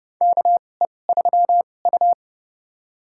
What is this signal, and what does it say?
Morse code, heard as a keyed tone, about 30 wpm, spaced as for 22 wpm, 700 Hz. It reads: KE3U
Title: USA Call Signs - 4 Characters - 2x Character Spacing - 30wpm